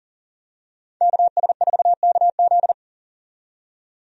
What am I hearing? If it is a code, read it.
Morse code, heard as a keyed tone, about 40 wpm, 700 Hz. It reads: KS4KZ
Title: USA Call Signs - 5 Characters - 40wpm